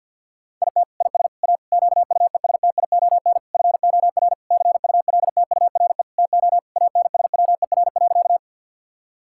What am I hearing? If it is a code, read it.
Morse code, heard as a keyed tone, about 46 wpm, 700 Hz. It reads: IT IS A QUESTION FOR CULTURE TO ANSWER.
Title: Sentences from Top 1000 Words - Review Part 6 - 50wpm